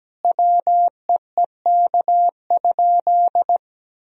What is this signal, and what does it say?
Morse code, heard as a keyed tone, about 17 wpm, 700 Hz. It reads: WEEK?
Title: Sentences from Top 1000 Words - Review Part 2 - 17wpm